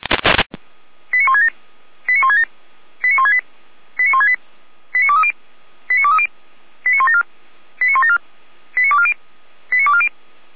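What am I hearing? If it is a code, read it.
Five-tone selective call (ZVEI), heard as quick runs of five tones, tones of about 80 ms. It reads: 89167 89167 89168 89168 89230 89230 89174 89174 89240 89240
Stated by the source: ZVEI five-tone call 89130 received by SDR.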